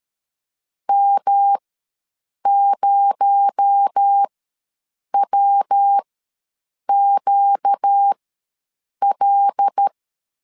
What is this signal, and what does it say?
Morse code, heard as a keyed tone, about 13 wpm, spaced as for 7 wpm, 800 Hz. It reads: M0WQL
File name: LEZIONI / 6 PKX0 / 6 PKX0 ws 12wpm 3448